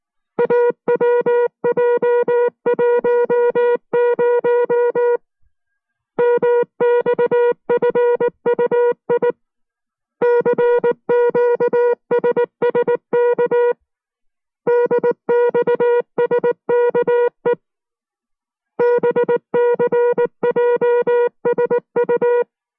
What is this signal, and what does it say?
Morse code, heard as a keyed tone, about 19 wpm, 480 Hz. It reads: AWJ10 MXFUI CQSSK DXSKE BCJSU